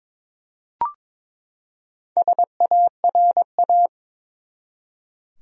Morse code, heard as a keyed tone, about 22 wpm, 700 Hz. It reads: SARA
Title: USA Common Names - Max 4 Letters - 22wpm